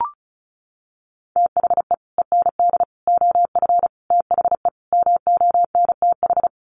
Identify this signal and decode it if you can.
Morse code, heard as a keyed tone, about 34 wpm, 700 Hz. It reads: THE END OF THE MONTH
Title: Sets of 5 Words Encore 35wpm